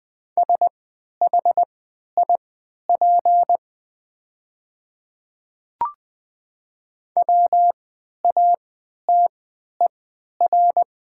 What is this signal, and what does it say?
Morse code, heard as a keyed tone, about 20 wpm, spaced as for 11 wpm, 700 Hz. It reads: SHIP WATER
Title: 3x Character Spacing - Top 500 Words - 20wpm